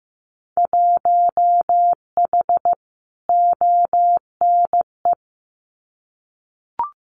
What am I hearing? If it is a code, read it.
Morse code, heard as a keyed tone, about 15 wpm, 700 Hz. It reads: 1H ONE
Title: Winter Field Day Exchanges 15wpm